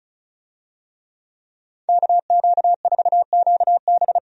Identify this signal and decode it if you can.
Morse code, heard as a keyed tone, about 36 wpm, 700 Hz. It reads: KQ4QB
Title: USA Call Signs - 5 Characters - Repeated 2x - 35wpm